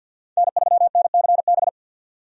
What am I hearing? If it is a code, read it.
Morse code, heard as a keyed tone, about 53 wpm, 700 Hz. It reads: N3NCB
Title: USA Call Signs - 5 Characters - 50wpm